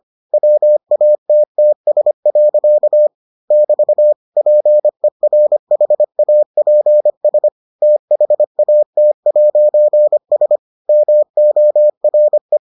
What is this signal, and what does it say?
Morse code, heard as a keyed tone, about 25 wpm, 600 Hz. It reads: WATTS. <BT> PERHAPS THAT'S MORE